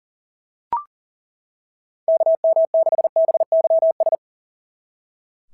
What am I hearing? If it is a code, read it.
Morse code, heard as a keyed tone, about 40 wpm, 690 Hz. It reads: KM6BYS
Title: SOTA QSO Practice 40wpm